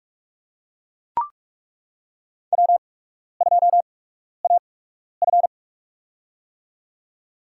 Morse code, heard as a keyed tone, about 43 wpm, spaced as for 12 wpm, 700 Hz. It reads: W2AF
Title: USA Call Signs - 4 Characters - 8x Character Spacing - 45wpm